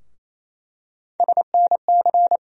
Morse code, heard as a keyed tone, about 28 wpm, 700 Hz. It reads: SNC
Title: USA Call Sign Postfixes - 3 Letters 28wpm